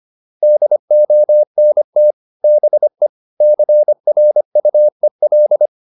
Morse code, heard as a keyed tone, about 25 wpm, 600 Hz. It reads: DONT BE CRUEL